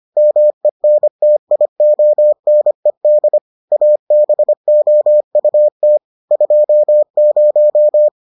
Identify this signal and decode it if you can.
Morse code, heard as a keyed tone, about 25 wpm, 600 Hz. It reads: MENTIONED ABOUT 20